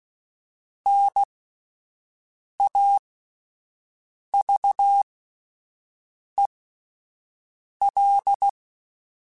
Morse code, heard as a keyed tone, about 16 wpm, spaced as for 5 wpm, 800 Hz. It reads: NAVEL